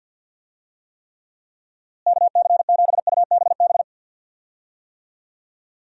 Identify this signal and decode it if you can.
Morse code, heard as a keyed tone, about 50 wpm, 700 Hz. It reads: KC7UBB
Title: USA Call Signs - 6 Characters - Repeated 2x - 50wpm